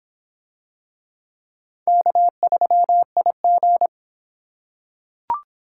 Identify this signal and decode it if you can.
Morse code, heard as a keyed tone, about 24 wpm, 700 Hz. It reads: K3IG